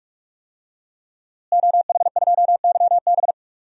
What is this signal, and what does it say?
Morse code, heard as a keyed tone, about 46 wpm, 700 Hz. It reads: OH2YB
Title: DXCC-100 Call Signs - 5 Characters - Repeated 2x - 45wpm